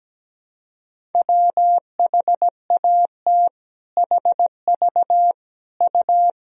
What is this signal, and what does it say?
Morse code, heard as a keyed tone, about 17 wpm, 700 Hz. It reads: WHAT HV U
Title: Sentences from Top 400 Words and Abbreviations 17wpm